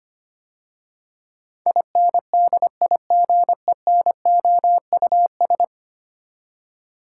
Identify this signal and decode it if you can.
Morse code, heard as a keyed tone, about 25 wpm, 700 Hz. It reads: INDIGENOUS
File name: 10 Letter Words - First 3 Letters Given - 25wpm